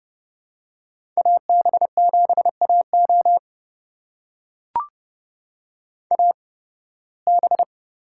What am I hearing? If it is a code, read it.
Morse code, heard as a keyed tone, about 30 wpm, spaced as for 20 wpm, 700 Hz. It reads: AB7AO AB
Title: USA Call Signs - 5 Characters - 8x Character Spacing - 30wpm